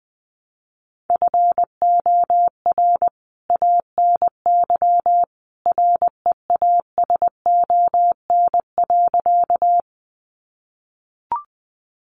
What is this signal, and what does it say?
Morse code, heard as a keyed tone, about 20 wpm, 700 Hz. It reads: FOR ANY REASON.
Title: A Sound Of Thunder 20wpm